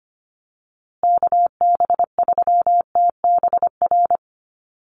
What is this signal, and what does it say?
Morse code, heard as a keyed tone, about 25 wpm, 700 Hz. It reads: KB3TBR
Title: USA Call Signs Repeated 3x 25wpm